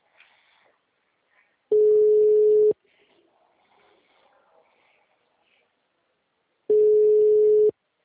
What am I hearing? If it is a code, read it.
Ringback tone (Europe, 425 Hz).